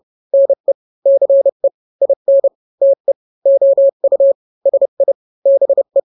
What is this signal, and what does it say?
Morse code, heard as a keyed tone, about 30 wpm, 550 Hz. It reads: NE CE IN TE OU SI BE
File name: n-grams / 2-grams 30 25 WPM